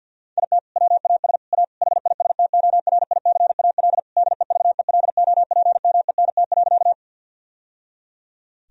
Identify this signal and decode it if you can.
Morse code, heard as a keyed tone, about 50 wpm, 700 Hz. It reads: IT WAS A HISTORICAL DEVELOPMENT.
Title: Sentences from Top 1000 Words - Review Part 4 - 50wpm